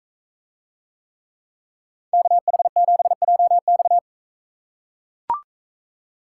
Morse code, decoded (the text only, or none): KH7JX